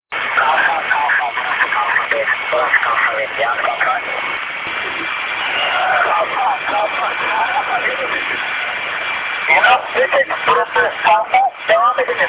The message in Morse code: none